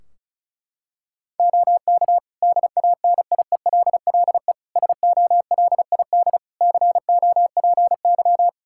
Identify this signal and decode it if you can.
Morse code, heard as a keyed tone, about 34 wpm, 700 Hz. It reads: OK DANIELLE SOLID COPY
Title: Practice QSO - Exchange 3 - More Info - 35wpm